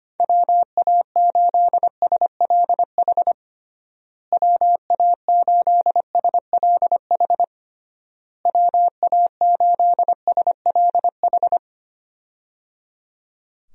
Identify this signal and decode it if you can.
Morse code, heard as a keyed tone, about 25 wpm, 700 Hz. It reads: WA8SLH WA8SLH WA8SLH
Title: USA Call Signs - 6 Characters - Repeated 3x - 25wpm